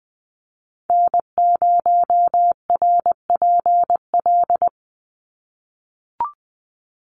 Morse code, heard as a keyed tone, about 20 wpm, 700 Hz. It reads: N0RPL
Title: USA Call Signs 20wpm